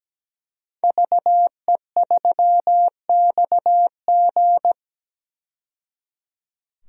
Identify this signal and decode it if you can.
Morse code, heard as a keyed tone, about 17 wpm, 700 Hz. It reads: VE3XG
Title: DXCC-100 Call Signs - 5 Characters - 17wpm